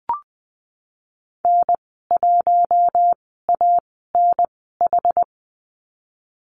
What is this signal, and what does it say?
Morse code, heard as a keyed tone, about 20 wpm, spaced as for 15 wpm, 700 Hz. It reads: N1ANH
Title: 062 Lesson 11g - Letter h - Call Signs 20wpm